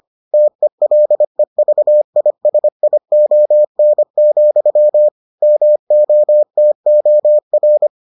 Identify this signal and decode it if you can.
Morse code, heard as a keyed tone, about 25 wpm, 600 Hz. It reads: TELEVISION, MOTOR